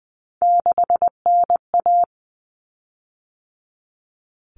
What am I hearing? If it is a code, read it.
Morse code, heard as a keyed tone, about 20 wpm, 700 Hz. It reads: 6NA
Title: Common Cut Numbers - 3-Digit 20wpm